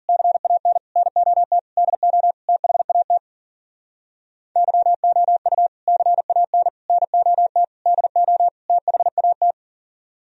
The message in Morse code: CAN NOT DO THAT YOU CAN NOT DO THAT

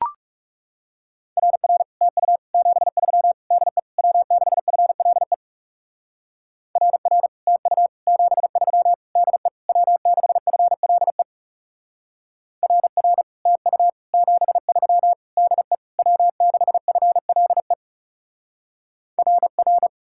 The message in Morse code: RR TU 73 DE W6FLE RR TU 73 DE W6FLE RR TU 73 DE W6FLE RR

